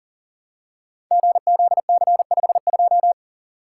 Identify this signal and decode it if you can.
Morse code, heard as a keyed tone, about 40 wpm, 700 Hz. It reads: GZC52
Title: Code Groups - 5 Letter:Numbers - 40wpm